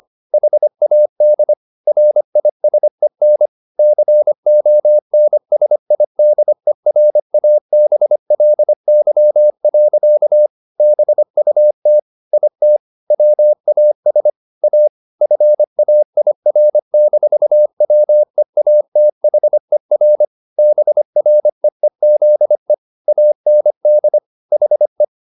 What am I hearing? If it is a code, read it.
Morse code, heard as a keyed tone, about 25 wpm, 600 Hz. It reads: HAD RISEN CONSIDERABLY. BUT IT WAS A FAIR-WEATHER BREEZE AND HE